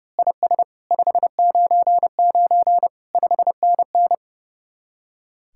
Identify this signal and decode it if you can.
Morse code, heard as a keyed tone, about 30 wpm, 700 Hz. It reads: IS 599 5NN
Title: Speed-Racing Common QSO Phrases Encore - 30wpm